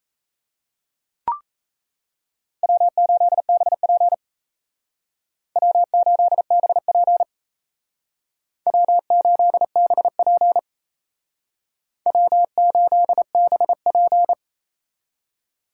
Morse code, heard as a keyed tone, about 36 wpm, 700 Hz. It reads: W8BP W8BP W8BP W8BP